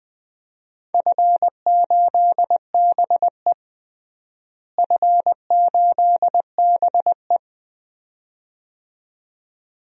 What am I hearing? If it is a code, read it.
Morse code, heard as a keyed tone, about 20 wpm, 700 Hz. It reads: F8BE F8BE
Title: DXCC-100 Call Signs - 4 Characters - Repeated 2x - 20wpm